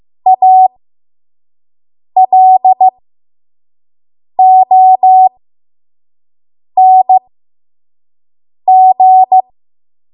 Morse code, heard as a keyed tone, about 15 wpm, spaced as for 5 wpm, 750 Hz. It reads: ALONG
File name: Morse / 05WPM